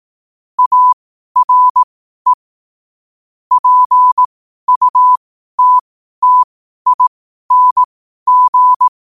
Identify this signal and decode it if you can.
Morse code, heard as a keyed tone, about 18 wpm, spaced as for 12 wpm, 1000 Hz. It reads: ARE PUTTING
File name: qotdfiles / 12wpm